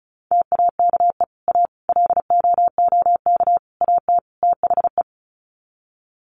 Morse code, heard as a keyed tone, about 34 wpm, 700 Hz. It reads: TAKE A LOOK AT THE
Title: Sets of 5 Words 35wpm